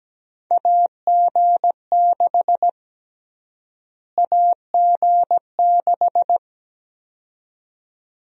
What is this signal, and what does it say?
Morse code, heard as a keyed tone, about 17 wpm, 700 Hz. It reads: AG6 AG6
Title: USA Call Sign Prefixes - Repeated 2x - 17wpm